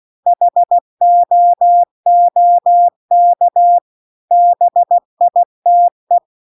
Morse code, heard as a keyed tone, about 16 wpm, 700 Hz. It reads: HOOK BITE
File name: CW-The-Big-Twoheaded-River-2-16WPM